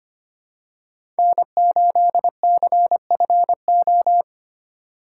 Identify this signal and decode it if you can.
Morse code, heard as a keyed tone, about 25 wpm, 700 Hz. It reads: N8CFO
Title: USA Call Signs - 5 Characters - 3x Character Spacing - 25wpm